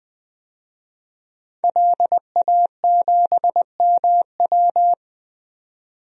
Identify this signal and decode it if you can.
Morse code, heard as a keyed tone, about 20 wpm, 700 Hz. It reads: LA7MW
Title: DXCC-100 Call Signs - 5 Characters - 20wpm